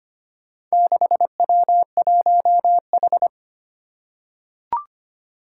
Morse code, heard as a keyed tone, about 25 wpm, 700 Hz. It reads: 6W1H
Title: Four Challenging Letter-Number Combinations 25wpm